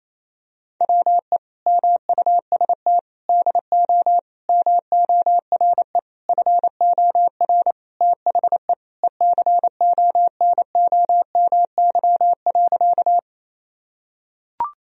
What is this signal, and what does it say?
Morse code, heard as a keyed tone, about 27 wpm, 700 Hz. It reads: WE MUST DO MORE FOR THE ECONOMY.
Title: Sentences from Top 1000 Words - Review Part 3 - 28wpm